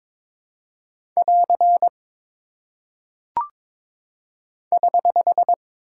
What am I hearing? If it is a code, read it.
Morse code, heard as a keyed tone, about 22 wpm, 700 Hz. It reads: <AR> <HH>
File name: practice / 22wpm / Single Punctuation-Prosign - 22wpm